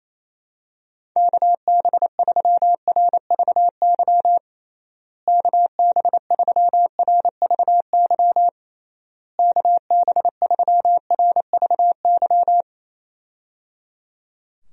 Morse code, heard as a keyed tone, about 28 wpm, 700 Hz. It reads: KB3RVY KB3RVY KB3RVY
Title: USA Call Signs - 6 Characters - Repeated 3x - 28wpm